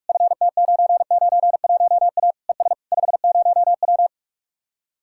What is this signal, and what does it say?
Morse code, heard as a keyed tone, about 49 wpm, 700 Hz. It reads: FT991A ES 50W